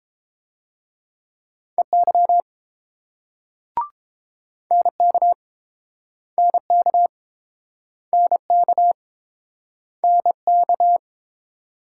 Morse code, heard as a keyed tone, about 31 wpm, 700 Hz. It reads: EY NK NK NK NK